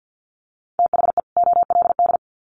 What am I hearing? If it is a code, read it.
Morse code, heard as a keyed tone, about 53 wpm, 700 Hz. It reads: THE OLD
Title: Sets of 2 Words 50wpm